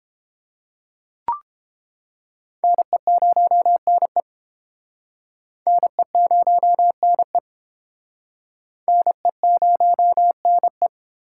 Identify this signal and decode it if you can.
Morse code, heard as a keyed tone, about 30 wpm, 700 Hz. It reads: NE0NE NE0NE NE0NE